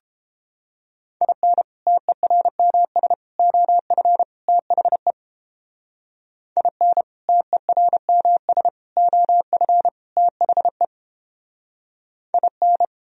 Morse code, heard as a keyed tone, about 31 wpm, 700 Hz. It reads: IN TERMS OF THE IN TERMS OF THE IN